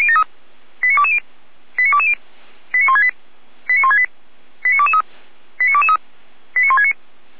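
Five-tone selective call (ZVEI), heard as quick runs of five tones, tones of about 70 ms. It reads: 89220 89220 89168 89168 89223 89223 89169
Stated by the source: ZVEI five-tone selective call 89167 received by SDR, fire brigade Orlamünde (feuerwehr-orlamuende).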